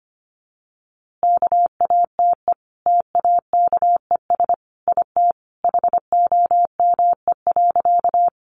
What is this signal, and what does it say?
Morse code, heard as a keyed tone, about 25 wpm, 700 Hz. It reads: KATE TAKES IT HOME.
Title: Easy Phrases 25wpm